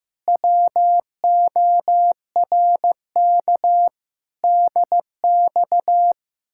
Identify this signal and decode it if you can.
Morse code, heard as a keyed tone, about 15 wpm, 700 Hz. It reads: WORK DX
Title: Speed-Racing Common QSO Phrases Encore - 15wpm